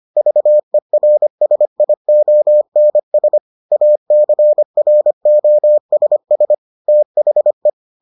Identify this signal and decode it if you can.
Morse code, heard as a keyed tone, about 25 wpm, 600 Hz. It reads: VERSIONS ACROSS THE